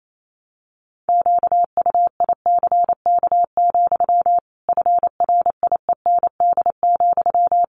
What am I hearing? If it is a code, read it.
Morse code, heard as a keyed tone, about 28 wpm, 700 Hz. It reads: QUICK, FRIEND,
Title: Sentences from Top 500 Words 28wpm